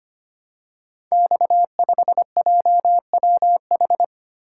Morse code, heard as a keyed tone, about 25 wpm, 700 Hz. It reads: X5JWH
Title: Code Groups - 5 Letter:Numbers - 25wpm